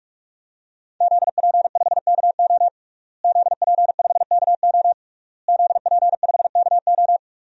Morse code, heard as a keyed tone, about 46 wpm, 700 Hz. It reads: ZP5KO ZP5KO ZP5KO
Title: DXCC-100 Call Signs - 5 Characters - Repeated 3x - 45wpm